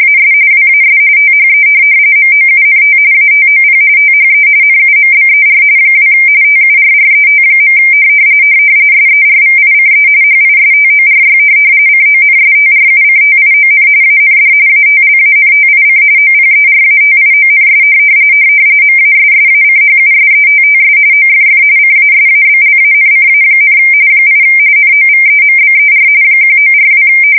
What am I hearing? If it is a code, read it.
RTTY, about 45.45 baud. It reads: AS WAS THE MOUNT ATOP THE NUMBER 2 TURRET. SERVICE HISTORY CONSTRUCTION AND ATLANTIC OPERATIONS MASSACHUSETTS OFF CASABLANCA, 8 NOVEMBER 1942
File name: fsk170 / USS-Massachusetts